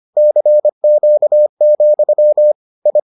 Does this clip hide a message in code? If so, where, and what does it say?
Yes: Morse code from 0.2 to 3.0 s, heard as a keyed tone, about 25 wpm, 600 Hz. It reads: CQ, I